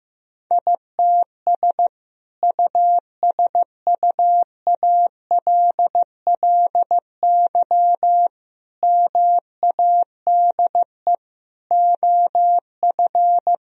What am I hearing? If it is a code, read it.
Morse code, heard as a keyed tone, about 15 wpm, 700 Hz. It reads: ITS USUALLY MADE OF
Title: Sentences from Top 1000 Words - Review Part 3 - 15wpm